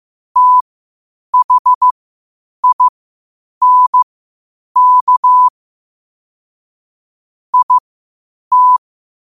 Morse code, heard as a keyed tone, about 15 wpm, spaced as for 8 wpm, 1000 Hz. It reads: THINK IT
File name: qotdfiles / 8wpm